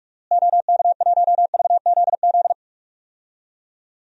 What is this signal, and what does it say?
Morse code, heard as a keyed tone, about 46 wpm, 700 Hz. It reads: OK1VZZ
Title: DXCC-100 Call Signs - 6 Characters - Repeated 2x - 45wpm